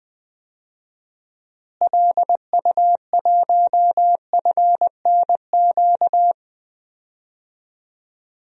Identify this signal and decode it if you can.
Morse code, heard as a keyed tone, about 20 wpm, 700 Hz. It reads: LU1FNQ
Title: DXCC-100 Call Signs - 6 Characters - Repeated 2x - 20wpm